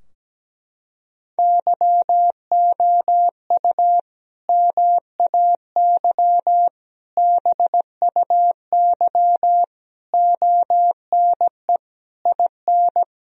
Morse code, heard as a keyed tone, about 17 wpm, 700 Hz. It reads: YOU MAY BUY ONE IN
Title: Sentences from Top 1000 Words - Review Part 4 - 17wpm